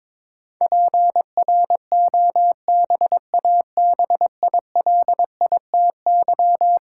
Morse code, heard as a keyed tone, about 22 wpm, 700 Hz. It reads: PROBABILITY
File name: practice / 22wpm / Eleven Letter Words - 22wpm